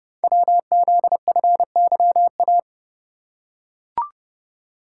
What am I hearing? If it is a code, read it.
Morse code, heard as a keyed tone, about 30 wpm, 700 Hz. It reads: WZFYA
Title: Code Groups - 5 Letters - 30wpm